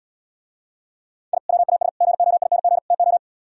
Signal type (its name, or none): Morse code